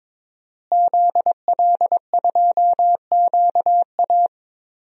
Morse code, heard as a keyed tone, about 22 wpm, 700 Hz. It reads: ZL2QA
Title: DXCC-100 Call Signs - 5 Characters - Repeated 3x - 22wpm